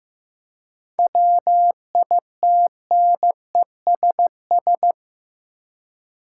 Morse code, heard as a keyed tone, about 15 wpm, 700 Hz. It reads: WITNESS
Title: Seven Letter Words - 15wpm